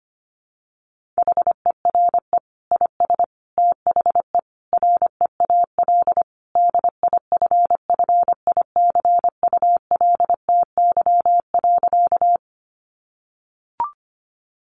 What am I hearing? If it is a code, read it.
Morse code, heard as a keyed tone, about 25 wpm, 700 Hz. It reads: HERE IS THE REAL DIFFICULTY.
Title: Sentences from Top 1500 Words 25wpm